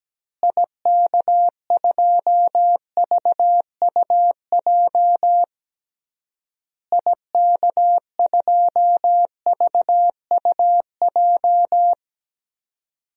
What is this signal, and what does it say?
Morse code, heard as a keyed tone, about 17 wpm, 700 Hz. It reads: IK2VUJ IK2VUJ
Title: DXCC-100 Call Signs - 6 Characters - Repeated 3x - 17wpm